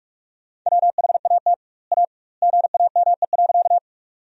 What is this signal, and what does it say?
Morse code, heard as a keyed tone, about 46 wpm, 700 Hz. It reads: WHAT A GAME.
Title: 2x Word-Spacing - Sentences from Top 500 Words - 45wpm